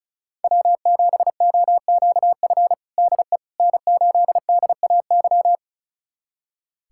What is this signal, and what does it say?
Morse code, heard as a keyed tone, about 36 wpm, 700 Hz. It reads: W7OQF DE N8DAY
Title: USA Callsign DE Callsign - 5 Characters - 35wpm